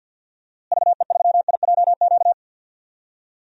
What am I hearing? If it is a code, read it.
Morse code, heard as a keyed tone, about 52 wpm, 700 Hz. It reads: VE3IJQ
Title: DXCC-100 Call Signs - 6 Characters - Repeated 2x - 50wpm